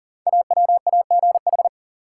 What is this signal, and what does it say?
Morse code, heard as a keyed tone, about 41 wpm, 700 Hz. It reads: AWAGH
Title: Code Groups - 5 Letter:Numbers - 40wpm